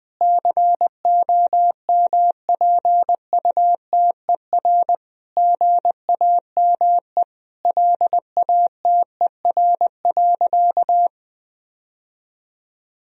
Morse code, heard as a keyed tone, about 21 wpm, 700 Hz. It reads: COMPUTER GAME LATER.
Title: Sentences from Top 1000 Words - Review Part 3 - 20wpm